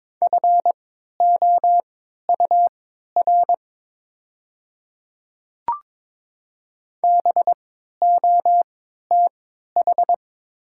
Morse code, heard as a keyed tone, about 22 wpm, spaced as for 12 wpm, 700 Hz. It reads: FOUR BOTH
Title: 3x Character Spacing - Top 500 Words - 22wpm